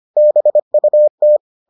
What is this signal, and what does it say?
Morse code, heard as a keyed tone, about 25 wpm, 600 Hz. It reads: BUT